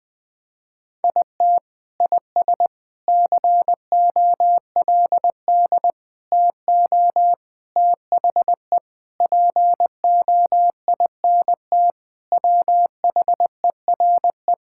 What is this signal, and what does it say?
Morse code, heard as a keyed tone, about 20 wpm, 700 Hz. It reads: IT IS COLD TO THE POINT WHERE
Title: Sentences with 4-Word Phrases - 20wpm